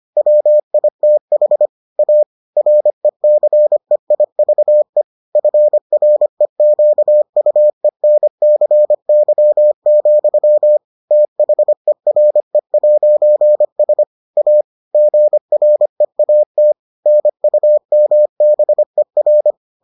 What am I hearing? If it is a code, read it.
Morse code, heard as a keyed tone, about 25 wpm, 600 Hz. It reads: WITH A RECEIVE FREQUENCY, THERE'S A GREAT NUMBER